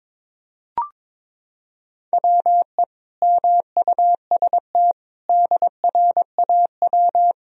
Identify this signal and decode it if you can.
Morse code, heard as a keyed tone, about 22 wpm, 700 Hz. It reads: WE MUST DRAW
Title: Sentences from Top 1000 Words - Review Part 1 - 22wpm